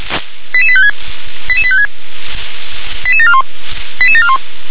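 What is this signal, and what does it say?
Five-tone selective call (ZVEI), heard as quick runs of five tones, tones of about 70 ms. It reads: 88856 88856 88741 88741
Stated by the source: ZVEI five-tone selective call received on SDR.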